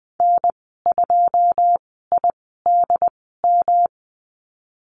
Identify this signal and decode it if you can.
Morse code, heard as a keyed tone, about 20 wpm, spaced as for 14 wpm, 700 Hz. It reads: N2IDM
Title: 179 Lesson 29g - Slash - Call Signs 20wpm